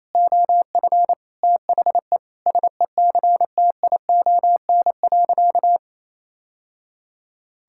Morse code, heard as a keyed tone, about 28 wpm, 700 Hz. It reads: OF THE SECTION.
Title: Sentences from Top 1000 Words - Review Part 5 - 28wpm